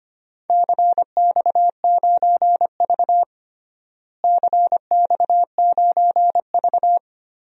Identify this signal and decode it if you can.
Morse code, heard as a keyed tone, about 25 wpm, 700 Hz. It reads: CX9V CX9V
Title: DXCC-100 Call Signs - 4 Characters - Repeated 3x - 25wpm